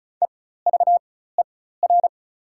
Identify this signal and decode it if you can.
Morse code, heard as a keyed tone, about 36 wpm, spaced as for 16 wpm, 690 Hz. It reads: EVER
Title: 4x Character Spacing - Top 500 Words - 35wpm